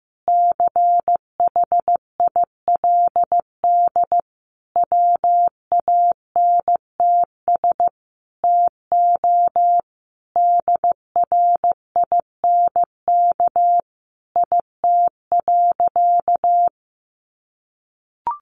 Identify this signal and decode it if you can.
Morse code, heard as a keyed tone, about 15 wpm, 700 Hz. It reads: CHILD WANTS TO DRINK IT.
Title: Additional Easy Phrases 15wpm